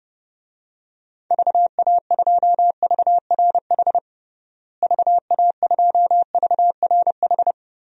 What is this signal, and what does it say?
Morse code, heard as a keyed tone, about 30 wpm, 700 Hz. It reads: VA2VRH VA2VRH
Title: DXCC-100 Call Signs - 6 Characters - Repeated 3x - 30wpm